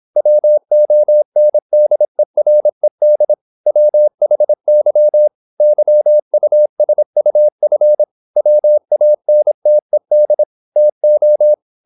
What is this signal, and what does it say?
Morse code, heard as a keyed tone, about 26 wpm, 600 Hz. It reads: WONDERED WHY YUSUF WANTED TO
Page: Never 26wpm | CW med Gnister